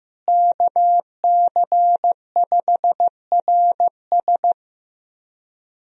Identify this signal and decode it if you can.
Morse code, heard as a keyed tone, about 15 wpm, 700 Hz. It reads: KC5RS
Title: USA Call Signs - 5 Characters - Repeated 2x - 15wpm